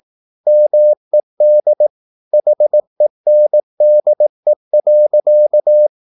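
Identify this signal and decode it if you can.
Morse code, heard as a keyed tone, about 18 wpm, 600 Hz. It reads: MED HENDE.
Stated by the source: Slow (QRS) Morse practice at 18 wpm, CW med Gnister.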